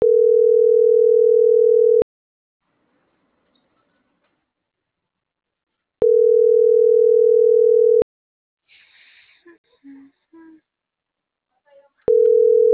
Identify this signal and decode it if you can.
Ringback tone (North America), 440 + 480 Hz.